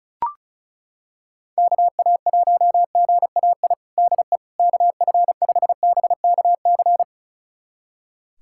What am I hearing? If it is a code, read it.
Morse code, heard as a keyed tone, about 36 wpm, 700 Hz. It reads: KA1GAI DE KF5BKC
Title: USA Callsign DE Callsign - 6 Characters - 35wpm